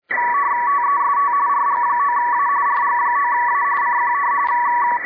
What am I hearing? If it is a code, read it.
RTTY, about 45.45 baud. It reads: TEST RG9A RG9A CQ